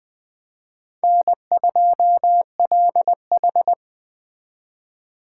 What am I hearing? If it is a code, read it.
Morse code, heard as a keyed tone, about 21 wpm, 700 Hz. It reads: N2LH
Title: Speed Racing - USA 1x2 Call Signs 17wpm